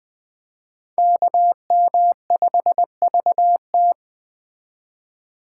Morse code, heard as a keyed tone, about 20 wpm, 700 Hz. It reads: KM5VT